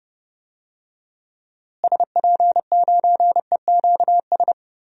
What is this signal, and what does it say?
Morse code, heard as a keyed tone, about 30 wpm, 700 Hz. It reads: SP9EQS
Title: DXCC-100 Call Signs - 6 Characters - Repeated 3x - 30wpm